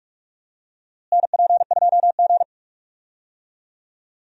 Morse code, heard as a keyed tone, about 46 wpm, 700 Hz. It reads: NP2G
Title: USA Call Signs - 4 Characters - Repeated 3x - 45wpm